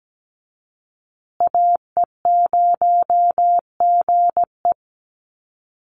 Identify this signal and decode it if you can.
Morse code, heard as a keyed tone, about 17 wpm, 700 Hz. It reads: AE0GE
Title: USA Call Signs Repeated 2x 17wpm